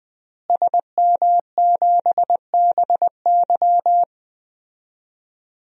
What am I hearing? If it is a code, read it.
Morse code, heard as a keyed tone, about 21 wpm, 700 Hz. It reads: SM7BY
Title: DXCC-100 Call Signs - 5 Characters - Repeated 2x - 20wpm